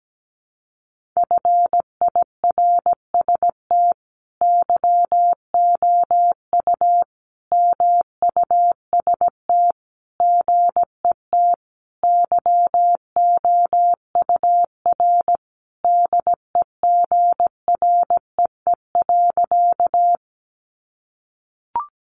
Morse code, heard as a keyed tone, about 17 wpm, 700 Hz. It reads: FIRST YOU MUST GET YOUR DEGREE.
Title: Sentences from Top 1500 Words 17wpm